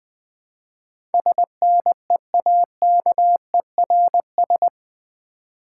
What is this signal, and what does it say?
Morse code, heard as a keyed tone, about 20 wpm, 700 Hz. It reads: SNEAKERS
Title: Easy Words with Hints - 20wpm